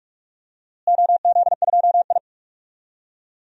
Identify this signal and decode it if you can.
Morse code, heard as a keyed tone, about 43 wpm, 700 Hz. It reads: OZ2I
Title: DXCC-100 Call Signs - 4 Characters - Repeated 2x - 45wpm